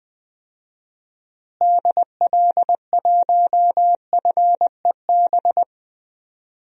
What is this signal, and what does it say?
Morse code, heard as a keyed tone, about 20 wpm, 700 Hz. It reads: DL1FEB
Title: DXCC-100 Call Signs - 6 Characters - Repeated 3x - 20wpm